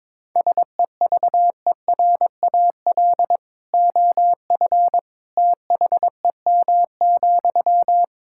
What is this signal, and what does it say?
Morse code, heard as a keyed tone, about 22 wpm, 700 Hz. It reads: SEVERAL OF THEM,
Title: Sentences from Top 1000 Words - Review Part 5 - 22wpm